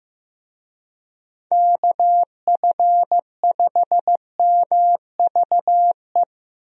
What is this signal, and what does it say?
Morse code, heard as a keyed tone, about 16 wpm, 700 Hz. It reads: KF5MVE
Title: USA Call Signs - 6 Characters - Repeated 3x - 15wpm